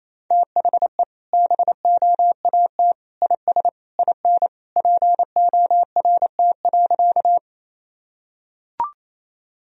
Morse code, heard as a keyed tone, about 28 wpm, 700 Hz. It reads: THE BOAT IS IN PORT.
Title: Sentences from Top 1000 Words - Review Part 1 - 28wpm